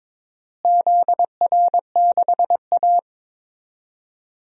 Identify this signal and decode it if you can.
Morse code, heard as a keyed tone, about 22 wpm, 700 Hz. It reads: ZR6A
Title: DXCC-100 Call Signs - 4 Characters - Repeated 2x - 22wpm